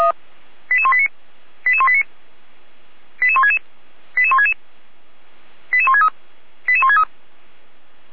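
Five-tone selective call (ZVEI), heard as quick runs of five tones, tones of about 80 ms. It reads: 88189 88189 88166 88166 88163 88163
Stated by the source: ZVEI five-tone call received by SDR.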